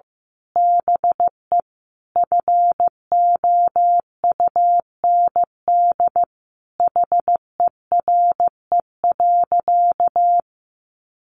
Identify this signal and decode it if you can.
Morse code, heard as a keyed tone, about 15 wpm, 700 Hz. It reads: BE FOUND HERE.
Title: Sentences from Top 1500 Words 15wpm